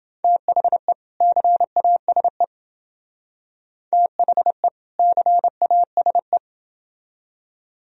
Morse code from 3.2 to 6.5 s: THE CASE

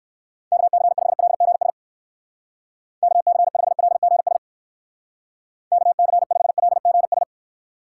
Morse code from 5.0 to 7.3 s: KC5LGS

